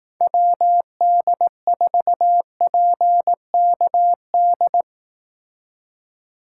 Morse code, heard as a keyed tone, about 18 wpm, 700 Hz. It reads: WD4PKD
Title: Speed Racing - USA Call Signs 15wpm